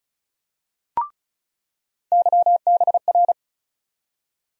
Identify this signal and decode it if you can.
Morse code, heard as a keyed tone, about 36 wpm, 700 Hz. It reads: YBR
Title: USA Call Sign Postfixes - 3 Letters 35wpm